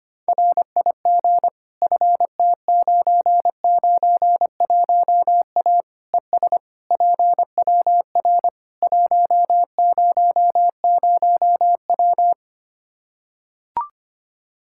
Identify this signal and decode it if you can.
Morse code, heard as a keyed tone, about 25 wpm, 700 Hz. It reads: RIG FT991A ES PWR 100W
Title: K1USN SST QSO Practice 25wpm